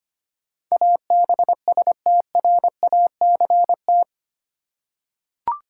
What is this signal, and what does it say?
Morse code, heard as a keyed tone, about 25 wpm, 700 Hz. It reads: ABSTRACT
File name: Eight Letter Words - 25wpm